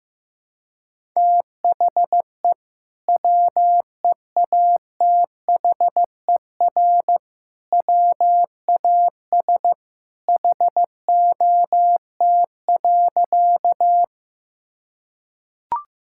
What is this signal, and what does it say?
Morse code, heard as a keyed tone, about 16 wpm, 700 Hz. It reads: THE WEATHER WAS HOT.
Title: Sentences from Top 1000 Words - Review Part 5 - 15wpm